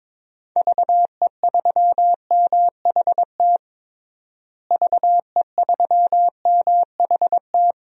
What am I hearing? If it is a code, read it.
Morse code, heard as a keyed tone, about 22 wpm, 700 Hz. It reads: VE3MHT VE3MHT
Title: DXCC-100 Call Signs - 6 Characters - Repeated 3x - 22wpm